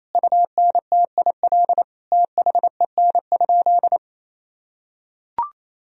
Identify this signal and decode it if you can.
Morse code, heard as a keyed tone, about 28 wpm, 700 Hz. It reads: UNTIL THEN?
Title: Sentences from Top 1000 Words - Review Part 1 - 28wpm